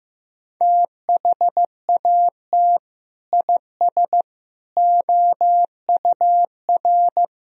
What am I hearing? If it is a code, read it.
Morse code, heard as a keyed tone, about 15 wpm, 700 Hz. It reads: THAT IS OUR
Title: Sentences from Top 1000 Words - Review Part 2 - 15wpm